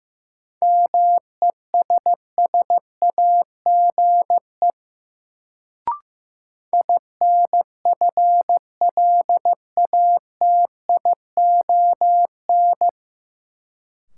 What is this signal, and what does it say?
Morse code, heard as a keyed tone, about 15 wpm, 700 Hz. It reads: MESSAGE INFLATION
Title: Sentences from Top 1100 Words - New Word Drill Alt - 15wpm